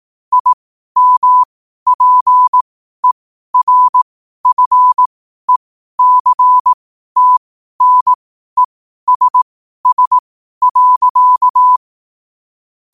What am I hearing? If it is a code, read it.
Morse code, heard as a keyed tone, about 18 wpm, spaced as for 12 wpm, 1000 Hz. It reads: IMPERFECTNESS.
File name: qotdfiles / 12wpm